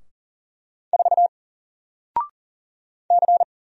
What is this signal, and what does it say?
Morse code, heard as a keyed tone, about 40 wpm, 700 Hz. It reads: 4 C
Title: Single Character (Letter:Number:Punctuation:Prosign) 40wpm